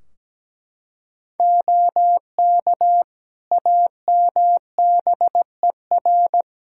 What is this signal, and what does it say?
Morse code, heard as a keyed tone, about 17 wpm, 700 Hz. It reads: OK AMBER
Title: Practice QSO - Exchange 3 - More Info - 17wpm